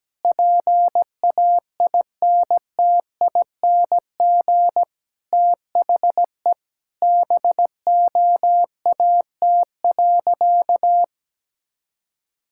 Morse code, heard as a keyed tone, about 17 wpm, 700 Hz. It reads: PAINTING THE BOAT.
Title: Sentences from Top 1000 Words - Review Part 6 - 17wpm